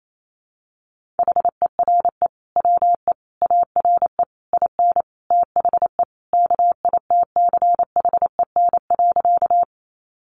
Morse code, heard as a keyed tone, about 28 wpm, 700 Hz. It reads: HERE WE ARE IN THE KITCHEN.
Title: Sentences from Top 1400 Words 28wpm